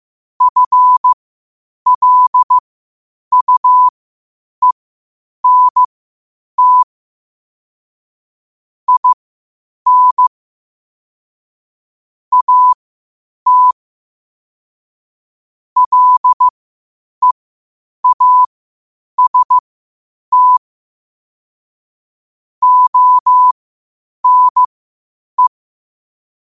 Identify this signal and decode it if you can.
Morse code, heard as a keyed tone, about 15 wpm, spaced as for 8 wpm, 1000 Hz. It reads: FLUENT IN AT LEAST ONE
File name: qotdfiles / 8wpm